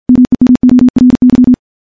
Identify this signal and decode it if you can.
Morse code, heard as a keyed tone, about 51 wpm, 250 Hz. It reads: MEMORY
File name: morse / morse-training-50WPM